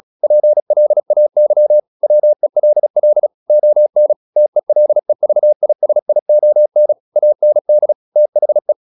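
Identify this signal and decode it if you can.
Morse code, heard as a keyed tone, about 36 wpm, 600 Hz. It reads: PLAY WELL ON TELEVISION AND THE
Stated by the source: Morse practice at 36 wpm, text of Ken Follett's novel Never, from CW med Gnister.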